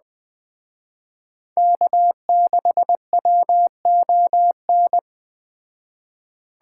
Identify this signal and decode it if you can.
Morse code, heard as a keyed tone, about 21 wpm, 700 Hz. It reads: K6WON